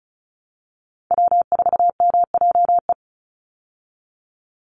Morse code, heard as a keyed tone, about 34 wpm, 700 Hz. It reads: W4MJE
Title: Call Signs Repeated 3x - 35wpm